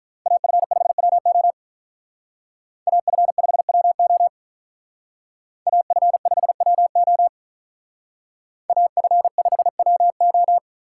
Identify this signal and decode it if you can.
Morse code, heard as a keyed tone, about 46 wpm, 700 Hz. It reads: AF5WO AF5WO AF5WO AF5WO